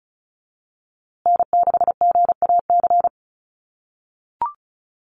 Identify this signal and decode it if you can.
Morse code, heard as a keyed tone, about 34 wpm, 700 Hz. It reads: N6GAC
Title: USA Call Signs Repeated 3x 35wpm